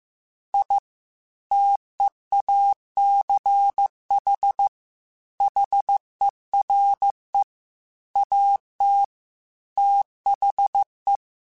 Morse code, heard as a keyed tone, about 15 wpm, 800 Hz. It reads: I TEACH HERE AT THE
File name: QSOs / QSOs-mp3s / WD0BGP 1@10